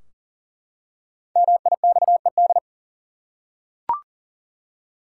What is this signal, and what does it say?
Morse code, heard as a keyed tone, about 40 wpm, 700 Hz. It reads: MIXED
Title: Five Letter Words - 40wpm